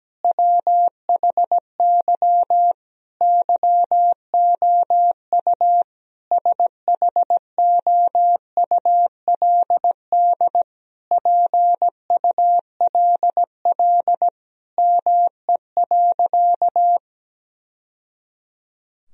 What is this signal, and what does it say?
Morse code, heard as a keyed tone, about 17 wpm, 700 Hz. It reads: WHY YOU SHOULD PULL ME.
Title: Sentences from Top 1000 Words - Review Part 2 - 17wpm